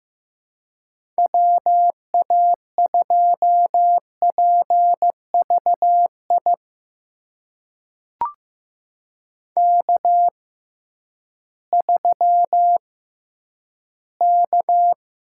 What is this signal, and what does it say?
Morse code, heard as a keyed tone, about 15 wpm, spaced as for 10 wpm, 700 Hz. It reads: WA2PVI K3K
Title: USA Call Signs - 6x Character Spacing - 15wpm